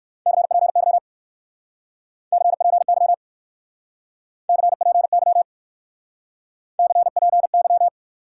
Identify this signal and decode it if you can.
Morse code, heard as a keyed tone, about 58 wpm, 700 Hz. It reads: CPY CPY CPY CPY